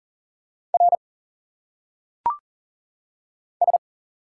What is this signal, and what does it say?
Morse code, heard as a keyed tone, about 39 wpm, 700 Hz. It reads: R S
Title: Single Letters - Phonetic Pronunciation 40wpm